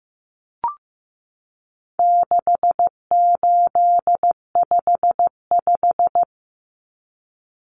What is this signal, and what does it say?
Morse code, heard as a keyed tone, about 15 wpm, 700 Hz. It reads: 6855
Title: Common Cut Numbers - 4-Digit 15wpm